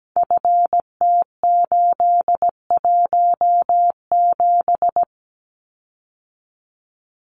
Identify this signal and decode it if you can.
Morse code, heard as a keyed tone, about 17 wpm, 700 Hz. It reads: FT817
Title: Common QSO Phrases Encore 17wpm